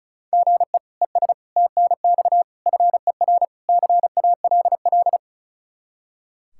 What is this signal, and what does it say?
Morse code, heard as a keyed tone, about 36 wpm, 700 Hz. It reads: GE ES TNX FER CALL
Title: K1USN SST QSO Practice 35wpm